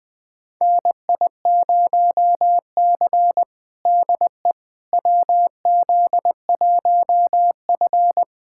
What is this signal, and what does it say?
Morse code, heard as a keyed tone, about 21 wpm, 700 Hz. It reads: NI0C DE WZ1F
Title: USA Callsign DE Callsign - 4 Characters - 20wpm